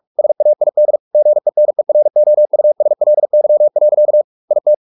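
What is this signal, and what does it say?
Morse code, heard as a keyed tone, about 44 wpm, 600 Hz. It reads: SAID GENEROUSLY. IT
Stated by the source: Morse practice at 45 wpm, the novel Never by Ken Follett in English, from CW med Gnister.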